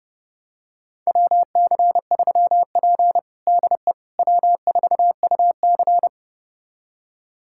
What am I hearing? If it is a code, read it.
Morse code, heard as a keyed tone, about 30 wpm, 700 Hz. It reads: WC3P DE W4UC
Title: USA Callsign DE Callsign - 4 Characters - 30wpm